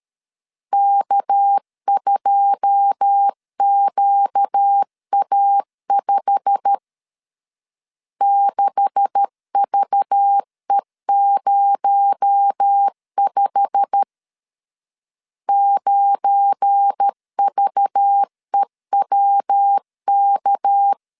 Morse code, heard as a keyed tone, about 13 wpm, 800 Hz. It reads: K2QA5 6VE05 9VEWK
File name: CW / DET / 1 9 1500C S B 12wpm 3819